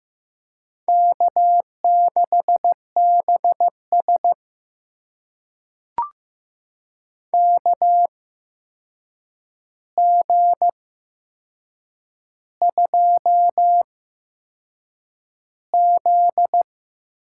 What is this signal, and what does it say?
Morse code, heard as a keyed tone, about 15 wpm, spaced as for 6 wpm, 700 Hz. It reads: K6BS KG2Z
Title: USA Call Signs - 4 Characters - 8x Character Spacing - 15wpm